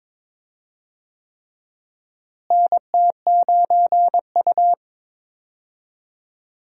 Morse code, heard as a keyed tone, about 21 wpm, 700 Hz. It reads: NT9U
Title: USA Call Signs - 4 Characters - Repeated 3x - 22wpm